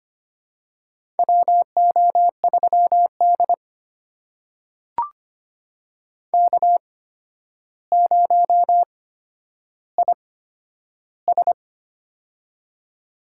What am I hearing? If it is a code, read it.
Morse code, heard as a keyed tone, about 25 wpm, spaced as for 11 wpm, 700 Hz. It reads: WO3D K0IS
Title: USA Call Signs - 4 Characters - 8x Character Spacing - 25wpm